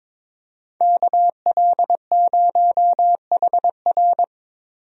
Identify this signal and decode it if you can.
Morse code, heard as a keyed tone, about 22 wpm, 700 Hz. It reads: KL0HR
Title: USA Call Signs - 5 Characters - Repeated 3x - 22wpm